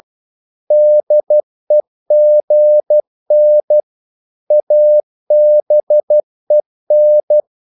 Morse code, heard as a keyed tone, about 12 wpm, 600 Hz. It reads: DEGN ABEN
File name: Korte ord DK 12wpm